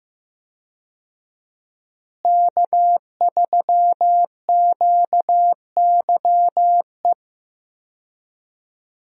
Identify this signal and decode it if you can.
Morse code, heard as a keyed tone, about 15 wpm, 700 Hz. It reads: K3QYE
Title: USA Call Signs - 5 Characters - Repeated 3x - 15wpm